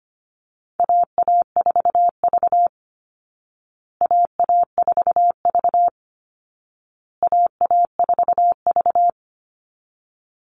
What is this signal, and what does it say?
Morse code, heard as a keyed tone, about 25 wpm, 700 Hz. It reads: AA4V AA4V AA4V
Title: Call Signs Repeated 2x - 25wpm